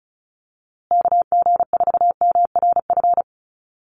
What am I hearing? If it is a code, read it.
Morse code, heard as a keyed tone, about 34 wpm, 700 Hz. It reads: KG4MRF
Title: USA Call Signs Repeated 2x 35wpm